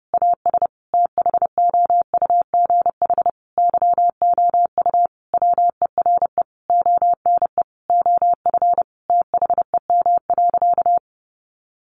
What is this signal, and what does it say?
Morse code, heard as a keyed tone, about 30 wpm, 700 Hz. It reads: AS THOUGH YOU WERE ONE OF THEM.
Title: Sentences from Top 1500 Words 30wpm